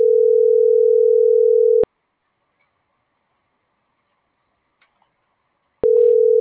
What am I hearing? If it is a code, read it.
Ringback tone (North America), 440 + 480 Hz.
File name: CENTRAL GENERAL / out alexander.sanchez